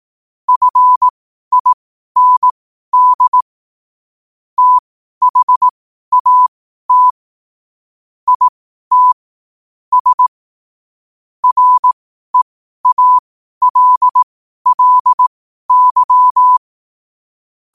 Morse code, heard as a keyed tone, about 18 wpm, spaced as for 12 wpm, 1000 Hz. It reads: FIND THAT IT S REALLY
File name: qotdfiles / 12wpm